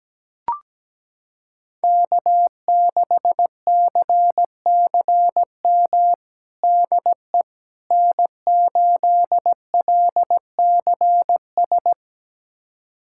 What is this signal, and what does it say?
Morse code, heard as a keyed tone, about 17 wpm, 700 Hz. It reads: K6CCM DE N8LCS
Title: USA Callsign DE Callsign - 5 Characters - 17wpm